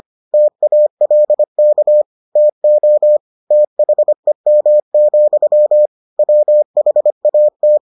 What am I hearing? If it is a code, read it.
Morse code, heard as a keyed tone, about 25 wpm, 600 Hz. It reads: TALK TO THEM, WHAT